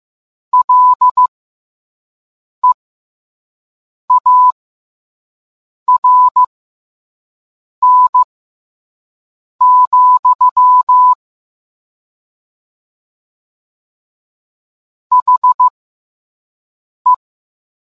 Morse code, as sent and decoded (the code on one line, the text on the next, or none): .-.. .- .-. -. --..-- / .... .
LEARN, HE